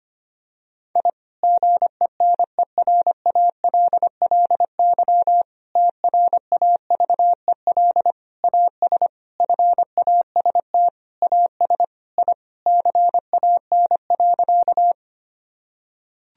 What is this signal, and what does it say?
Morse code, heard as a keyed tone, about 25 wpm, 700 Hz. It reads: I GENERALLY TRAVEL AS FAST AS I CAN.
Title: Sentences from Top 1000 Words - Review Part 4 - 25wpm